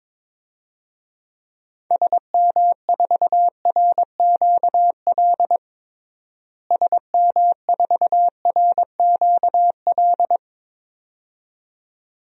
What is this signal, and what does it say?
Morse code, heard as a keyed tone, about 22 wpm, 700 Hz. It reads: SM4RQL SM4RQL
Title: DXCC-100 Call Signs - 6 Characters - Repeated 2x - 22wpm